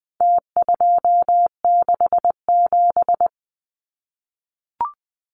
Morse code, heard as a keyed tone, about 20 wpm, 700 Hz. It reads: T267
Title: Numbers - Contest Style - 4-Digit 20wpm